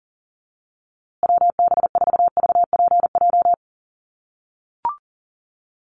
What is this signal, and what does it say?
Morse code, heard as a keyed tone, about 40 wpm, 700 Hz. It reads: WB4VPJ
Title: USA Call Signs Repeated 2x 40wpm